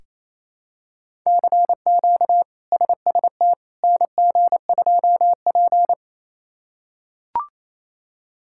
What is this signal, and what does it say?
Morse code, heard as a keyed tone, about 28 wpm, 700 Hz. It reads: CQ SST NG2P
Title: K1USN SST QSO Practice - 28wpm